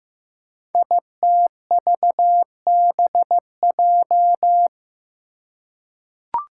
Morse code, heard as a keyed tone, about 15 wpm, 700 Hz. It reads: ITVBJ
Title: Code Groups - 5 Letters - 15wpm